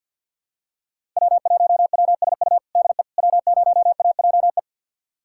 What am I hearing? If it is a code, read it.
Morse code, heard as a keyed tone, about 48 wpm, 700 Hz. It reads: W1WSU DE W0AJE
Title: USA Callsign DE Callsign - 5 Characters - 50wpm